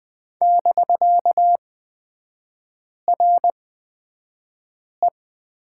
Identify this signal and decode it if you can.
Morse code, heard as a keyed tone, about 20 wpm, spaced as for 5 wpm, 700 Hz. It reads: <BK>RE
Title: 250 Lesson 40e - BK - Three Characters 20wpm